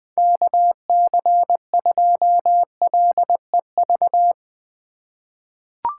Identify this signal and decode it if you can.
Morse code, heard as a keyed tone, about 20 wpm, 700 Hz. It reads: KC2LEV
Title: USA Call Signs - 4x Character Spacing - 20wpm